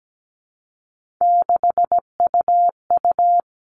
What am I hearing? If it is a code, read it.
Morse code, heard as a keyed tone, about 17 wpm, 700 Hz. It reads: 6UU
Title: All Cut Numbers - 3-Digit 17wpm